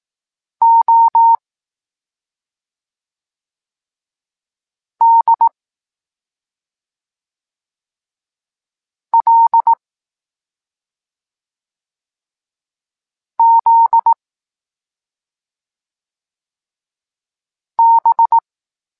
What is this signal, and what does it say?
Morse code, heard as a keyed tone, about 18 wpm, 920 Hz. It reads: O D L Z B
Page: Tekst 23.3 – Ciąg dalszy doskonalenia odbioru poznanych liter